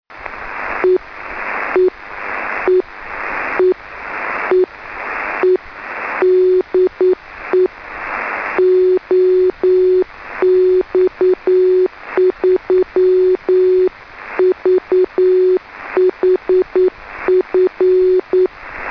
Morse code, heard as a keyed tone, about 9 wpm, 370 Hz. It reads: E E E E E E DE OX3VHF
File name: 2m beac 144,418 Mhz